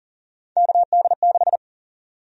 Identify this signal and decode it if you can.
Morse code, heard as a keyed tone, about 40 wpm, 700 Hz. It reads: KD6
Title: USA Call Sign Prefixes - Repeated 2x - 40wpm